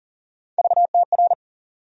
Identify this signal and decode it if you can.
Morse code, heard as a keyed tone, about 40 wpm, 700 Hz. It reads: VTR
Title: USA Call Sign Postfixes - 3 Letters 40wpm